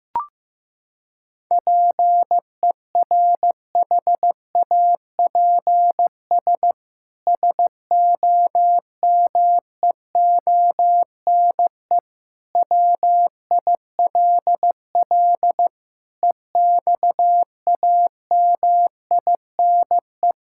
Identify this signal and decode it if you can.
Morse code, heard as a keyed tone, about 15 wpm, 700 Hz. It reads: PERHAPS SOMEONE WILL EXAMINE
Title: Sentences from Top 1600 Words 15wpm